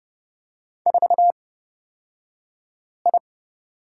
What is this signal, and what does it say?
Morse code, heard as a keyed tone, about 30 wpm, spaced as for 5 wpm, 700 Hz. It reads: 4I
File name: Lesson 38d - Letter q - Two Characters 30wpm